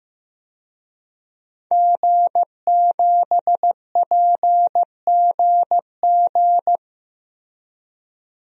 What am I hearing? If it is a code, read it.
Morse code, heard as a keyed tone, about 15 wpm, 700 Hz. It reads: G7PGG
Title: DXCC-100 Call Signs - 5 Characters - 15wpm